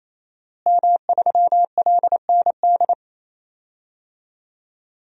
Morse code, heard as a keyed tone, about 28 wpm, 690 Hz. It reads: M3LND
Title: DXCC-100 Call Signs - 5 Characters - Repeated 3x - 28wpm